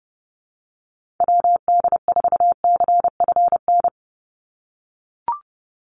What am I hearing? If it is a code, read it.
Morse code, heard as a keyed tone, about 30 wpm, 700 Hz. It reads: WD4CFN
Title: NA SOTA Chaser Callsigns 30wpm